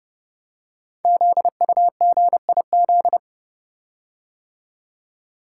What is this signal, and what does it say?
Morse code, heard as a keyed tone, about 29 wpm, 700 Hz. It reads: ZUGIZ
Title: Code Groups - 5 Letter:Numbers - 30wpm